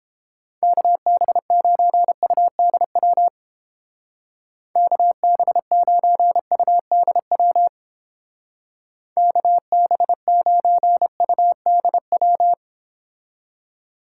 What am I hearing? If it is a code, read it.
Morse code, heard as a keyed tone, about 28 wpm, 700 Hz. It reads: KB9UDW KB9UDW KB9UDW